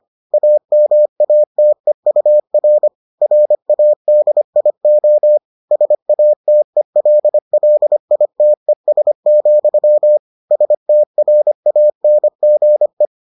Morse code, heard as a keyed tone, about 25 wpm, 600 Hz. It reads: AMATEUR RADIO SATELLITES, STRANGE